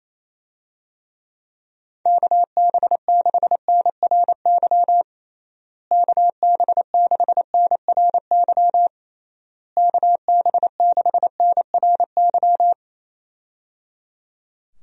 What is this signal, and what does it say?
Morse code, heard as a keyed tone, about 28 wpm, 700 Hz. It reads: KB6NRY KB6NRY KB6NRY
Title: USA Call Signs - 6 Characters - Repeated 3x - 28wpm